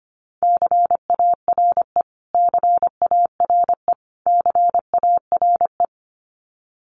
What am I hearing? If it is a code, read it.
Morse code, heard as a keyed tone, about 25 wpm, 700 Hz. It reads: CARE CARE CARE
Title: Top 300 Words Repeated 25wpm